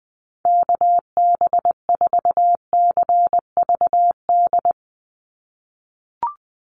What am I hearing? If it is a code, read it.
Morse code, heard as a keyed tone, about 20 wpm, 700 Hz. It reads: KB4CVD
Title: USA Call Signs 20wpm